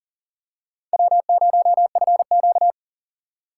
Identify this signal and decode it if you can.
Morse code, heard as a keyed tone, about 38 wpm, 700 Hz. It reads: W0FQ
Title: USA Call Signs - 4 Characters - 4x Character Spacing - 40wpm